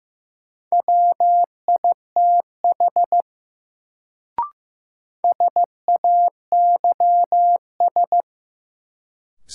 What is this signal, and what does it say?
Morse code, heard as a keyed tone, about 15 wpm, 700 Hz. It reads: WITH SAYS
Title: Sets of 2 Words - New Word Drill - 15wpm